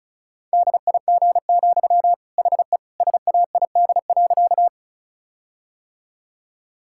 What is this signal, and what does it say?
Morse code, heard as a keyed tone, about 34 wpm, 700 Hz. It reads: DIG, HE SAID.
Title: Sentences from Top 1600 Words 35wpm